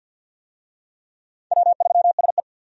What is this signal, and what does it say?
Morse code, heard as a keyed tone, about 52 wpm, 700 Hz. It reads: W3SE
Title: USA Call Signs - 4 Characters - 50wpm